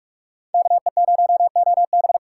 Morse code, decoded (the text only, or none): KE0OB